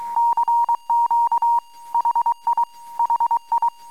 Morse code, heard as a keyed tone, about 23 wpm, 910 Hz. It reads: CQ HI HI
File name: amsat / cw